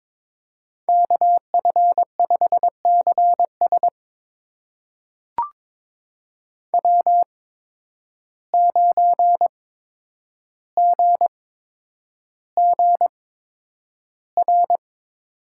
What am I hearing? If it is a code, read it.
Morse code, heard as a keyed tone, about 22 wpm, spaced as for 9 wpm, 700 Hz. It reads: KF5CS W9GGR
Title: USA Call Signs - 5 Characters - 8x Character Spacing - 22wpm